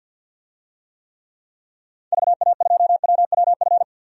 Morse code, heard as a keyed tone, about 47 wpm, 700 Hz. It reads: VA2WWF
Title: DXCC-100 Call Signs - 6 Characters - 50wpm